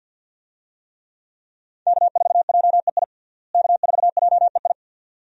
Morse code, heard as a keyed tone, about 46 wpm, 700 Hz. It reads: K4JEI K4JEI
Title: USA Call Signs - 5 Characters - Repeated 2x - 50wpm